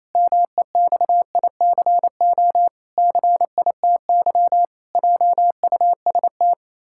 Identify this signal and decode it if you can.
Morse code, heard as a keyed tone, about 27 wpm, 700 Hz. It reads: MEXICO CITY JUST
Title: Pangram Challenge v2 - 28wpm